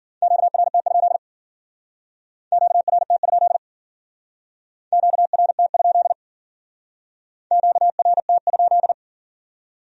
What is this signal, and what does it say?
Morse code, heard as a keyed tone, about 53 wpm, 700 Hz. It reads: QRT? QRT? QRT? QRT?